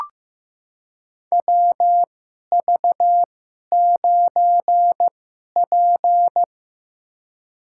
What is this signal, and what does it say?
Morse code, heard as a keyed tone, about 15 wpm, spaced as for 11 wpm, 700 Hz. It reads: WV9P
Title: USA Call Signs - 4 Characters - 2x Character Spacing - 15wpm